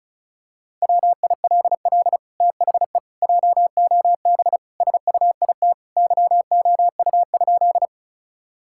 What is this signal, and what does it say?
Morse code, heard as a keyed tone, about 36 wpm, 700 Hz. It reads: WILL THE JOB SUIT YOU?
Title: Sentences from Top 1600 Words 35wpm